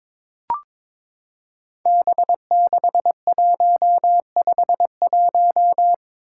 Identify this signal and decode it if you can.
Morse code, heard as a keyed tone, about 22 wpm, 700 Hz. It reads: B6151
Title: Five Challenging Letter-Number Combinations 22wpm